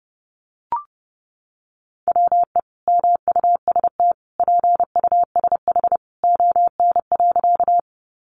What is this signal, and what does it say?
Morse code, heard as a keyed tone, about 30 wpm, 700 Hz. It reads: WE MUST PUSH ON.
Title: Sentences from Top 1300 Words 30wpm